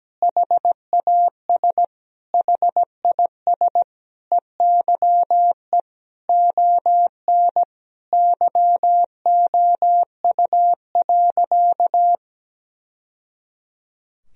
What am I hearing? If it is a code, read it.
Morse code, heard as a keyed tone, about 17 wpm, 700 Hz. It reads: HAS HIS EYE ON YOU.
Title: Sentences from Top 1000 Words - Review Part 1 - 17wpm